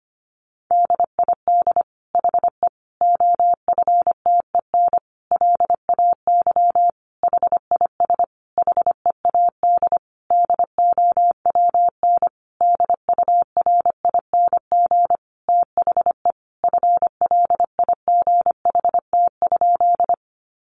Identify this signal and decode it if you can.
Morse code, heard as a keyed tone, about 25 wpm, 700 Hz. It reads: DID HE OFTEN LAY HIS HEAD DOWN DURING THE FLIGHT?
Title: Sentences from Top 1000 Words 25wpm